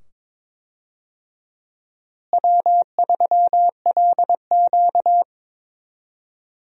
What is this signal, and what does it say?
Morse code, heard as a keyed tone, about 22 wpm, 700 Hz. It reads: W3LQ
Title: USA Call Signs - 4 Characters - Repeated 2x - 22wpm